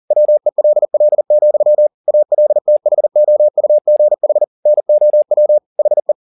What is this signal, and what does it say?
Morse code, heard as a keyed tone, about 40 wpm, 600 Hz. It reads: WELL, ALTHOUGH NOW HE
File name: the old man and the sea 2 40wpm